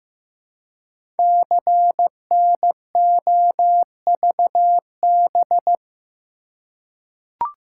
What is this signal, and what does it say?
Morse code, heard as a keyed tone, about 15 wpm, 700 Hz. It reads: CNOVB
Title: Code Groups - 5 Letters - 15wpm